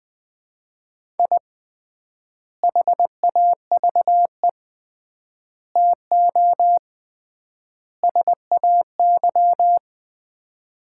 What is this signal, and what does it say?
Morse code, heard as a keyed tone, about 20 wpm, 700 Hz. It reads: I HAVE TO SAY
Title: 3x Word-Spacing - Sets of 4 Words - 20wpm